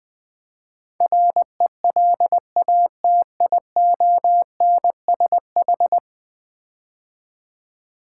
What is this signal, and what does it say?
Morse code, heard as a keyed tone, about 20 wpm, 700 Hz. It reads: RELATIONSH
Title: Build A Word - 5 Letters At A Time - 12-Letter Words - 20wpm